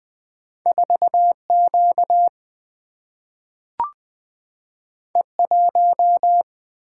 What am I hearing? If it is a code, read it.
Morse code, heard as a keyed tone, about 20 wpm, 700 Hz. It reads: 4Q E1
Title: Single-Letter Single-Number 20wpm